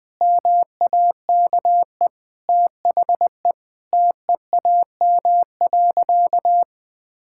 Morse code, heard as a keyed tone, about 20 wpm, 700 Hz. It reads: MAKE THE TEAM.
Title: Sentences with 3-Word Phrases - Spoken Beginning - 20wpm